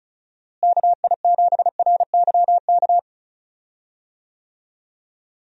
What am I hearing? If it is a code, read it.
Morse code, heard as a keyed tone, about 36 wpm, 700 Hz. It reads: KI7RYK
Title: USA Call Signs - 6 Characters - Repeated 2x - 35wpm